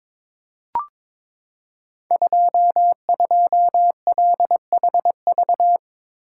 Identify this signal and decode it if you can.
Morse code, heard as a keyed tone, about 22 wpm, 690 Hz. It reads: 22LHV
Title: Code Groups - 5 Letter:Numbers - 22wpm